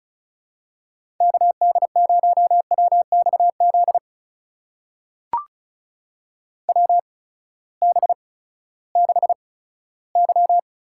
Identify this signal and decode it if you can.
Morse code, heard as a keyed tone, about 36 wpm, spaced as for 18 wpm, 700 Hz. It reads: KD0WXZ WB6Y
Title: USA Call Signs - 6 Characters - 8x Character Spacing - 35wpm